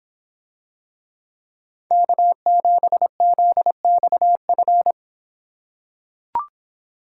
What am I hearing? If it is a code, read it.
Morse code, heard as a keyed tone, about 26 wpm, 700 Hz. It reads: K7ZXF